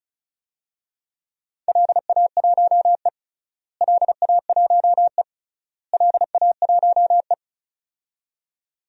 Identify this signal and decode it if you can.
Morse code, heard as a keyed tone, about 36 wpm, 700 Hz. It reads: LA1E LA1E LA1E
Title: DXCC-100 Call Signs - 4 Characters - Repeated 3x - 35wpm